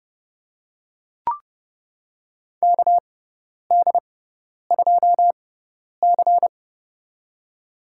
Morse code, heard as a keyed tone, about 30 wpm, spaced as for 10 wpm, 700 Hz. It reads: KD2C
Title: USA Call Signs - 4 Characters - 6x Character Spacing - 30wpm